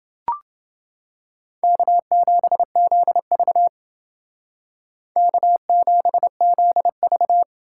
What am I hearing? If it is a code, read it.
Morse code, heard as a keyed tone, about 28 wpm, 700 Hz. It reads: K7ZV K7ZV